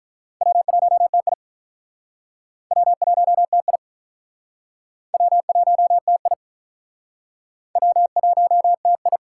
Morse code, heard as a keyed tone, about 45 wpm, 700 Hz. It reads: W1TI W1TI W1TI W1TI